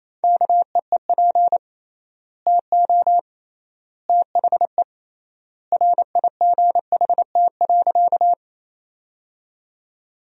Morse code, heard as a keyed tone, about 28 wpm, 700 Hz. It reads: KEEP TO THE RIGHT.
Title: 3x Word-Spacing - Sentences from Top 300 Words - 28wpm